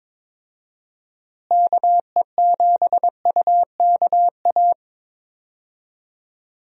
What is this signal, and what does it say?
Morse code, heard as a keyed tone, about 21 wpm, 700 Hz. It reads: KE7UKA
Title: USA Call Signs - 6 Characters - Repeated 2x - 22wpm